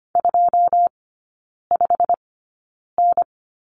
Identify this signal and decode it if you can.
Morse code, heard as a keyed tone, about 24 wpm, spaced as for 9 wpm, 700 Hz. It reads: 25N
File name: Lesson 15e - Number 5 - Three Characters 25wpm